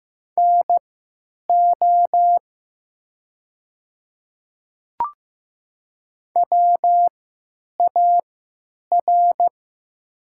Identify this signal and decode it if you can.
Morse code, heard as a keyed tone, about 15 wpm, spaced as for 8 wpm, 700 Hz. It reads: NO WAR
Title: 3x Character Spacing - Top 500 Words - 15wpm